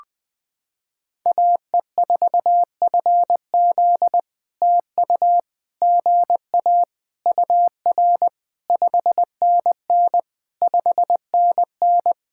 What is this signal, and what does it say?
Morse code, heard as a keyed tone, about 20 wpm, 700 Hz. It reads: AE4FZ TU GA UR 5NN 5NN
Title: POTA QSO Practice - 20wpm